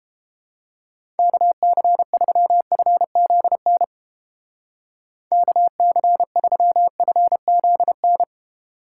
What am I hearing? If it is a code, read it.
Morse code, heard as a keyed tone, about 29 wpm, 700 Hz. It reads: KC3FZN KC3FZN